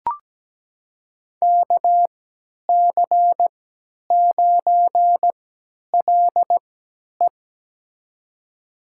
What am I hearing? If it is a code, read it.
Morse code, heard as a keyed tone, about 17 wpm, spaced as for 10 wpm, 700 Hz. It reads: KC9LE
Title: USA Call Signs - 3x Character Spacing - 17wpm